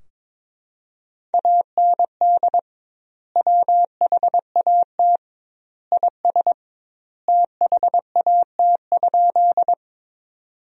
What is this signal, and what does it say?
Morse code, heard as a keyed tone, about 22 wpm, 700 Hz. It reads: AND WHAT IS THAT?
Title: 2x Word-Spacing - Sentences from Top 100 Words - 22wpm